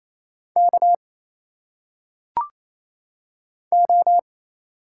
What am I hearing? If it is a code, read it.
Morse code, heard as a keyed tone, about 27 wpm, 700 Hz. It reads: K O
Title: Single Letters - Phonetic Pronunciation 28wpm